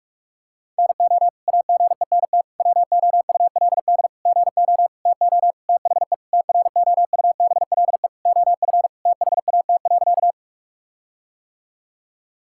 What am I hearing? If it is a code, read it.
Morse code, heard as a keyed tone, about 42 wpm, 700 Hz. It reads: NO AGENT WOULD GO TO THE TROUBLE OF THAT.
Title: Sentences from Top 1600 Words 45wpm